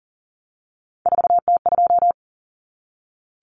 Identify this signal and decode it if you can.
Morse code, heard as a keyed tone, about 40 wpm, 700 Hz. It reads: 4T2
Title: Common Cut Numbers - 3-Digit 40wpm